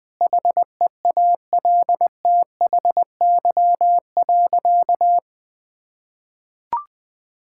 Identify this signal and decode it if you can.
Morse code, heard as a keyed tone, about 20 wpm, 700 Hz. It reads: HEALTHY.
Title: Sentences from Top 1000 Words - Review Part 5 - 20wpm